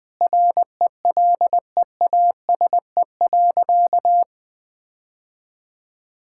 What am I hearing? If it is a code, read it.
Morse code, heard as a keyed tone, about 20 wpm, 700 Hz. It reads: RELEASE.
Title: Sentences from Top 1600 Words 20wpm